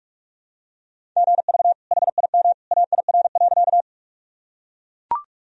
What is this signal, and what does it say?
Morse code, heard as a keyed tone, about 41 wpm, 700 Hz. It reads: GV HIM AIR.
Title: Sentences from Top 200 Words and Abbreviations 45wpm